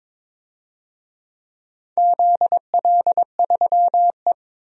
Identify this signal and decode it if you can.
Morse code, heard as a keyed tone, about 21 wpm, 700 Hz. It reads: ZL3E
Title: DXCC-100 Call Signs - 4 Characters - Repeated 3x - 22wpm